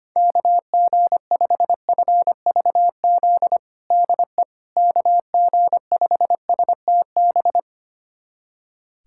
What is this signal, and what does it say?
Morse code, heard as a keyed tone, about 25 wpm, 700 Hz. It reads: KG5FVZ DE KG5STB
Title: USA Callsign DE Callsign - 6 Characters - 25wpm